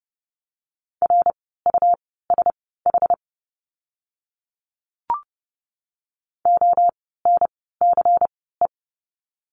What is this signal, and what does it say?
Morse code, heard as a keyed tone, about 30 wpm, spaced as for 17 wpm, 700 Hz. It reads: RUSH ONCE
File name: Lesson 17f - Letter u - Words 30wpm